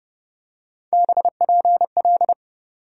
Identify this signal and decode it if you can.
Morse code, heard as a keyed tone, about 30 wpm, 700 Hz. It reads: BPL
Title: USA Call Sign Postfixes - 3 Letters 30wpm